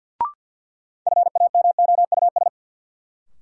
Morse code, heard as a keyed tone, about 47 wpm, 700 Hz. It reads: FAMOUS
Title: Sentences from Top 1500 Words - New Word Drill - 50wpm